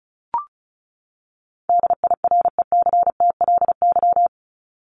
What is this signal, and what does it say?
Morse code, heard as a keyed tone, about 34 wpm, 700 Hz. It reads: DIRECTLY
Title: LY Suffix - Top 100 Words with Base Words 35wpm